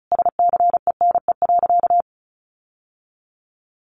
Morse code, heard as a keyed tone, about 34 wpm, 700 Hz. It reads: SCENE.
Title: Sentences from Top 1100 Words 35wpm